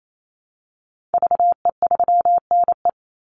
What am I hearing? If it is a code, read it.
Morse code, heard as a keyed tone, about 28 wpm, 700 Hz. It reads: VE3NE
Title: Call Signs Repeated 2x - 28wpm